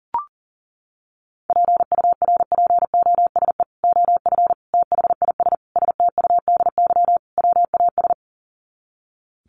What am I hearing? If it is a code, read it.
Morse code, heard as a keyed tone, about 40 wpm, 700 Hz. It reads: PURPOSE OF THIS STUDY WAS
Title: Sets of 5 Words 40wpm